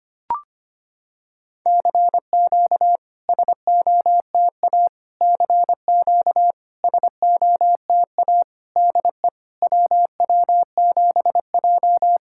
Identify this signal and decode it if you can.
Morse code, heard as a keyed tone, about 25 wpm, 690 Hz. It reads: CQ SOTA CQ SOTA DE WW7J
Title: SOTA QSO Practice 25wpm